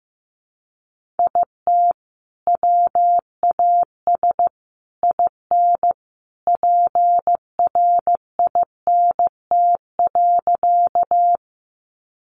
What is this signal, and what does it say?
Morse code, heard as a keyed tone, about 15 wpm, 700 Hz. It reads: IT WAS IN PRINT.
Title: Sentences from Top 1400 Words 15wpm